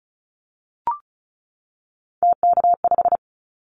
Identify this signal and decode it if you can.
Morse code, heard as a keyed tone, about 34 wpm, 700 Hz. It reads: TK5
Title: Callsign Prefixes for Top 100 DXCC - 35wpm